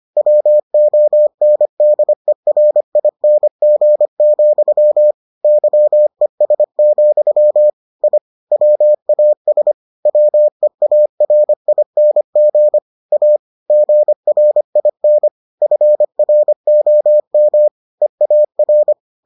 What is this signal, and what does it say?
Morse code, heard as a keyed tone, about 25 wpm, 600 Hz. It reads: WONDERING, YES, I WAS WEARING A GRIN FROM EAR